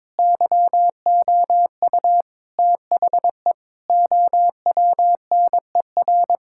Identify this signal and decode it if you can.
Morse code, heard as a keyed tone, about 22 wpm, 700 Hz. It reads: YOU THE OWNER
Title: Sentences from Top 1600 Words 22wpm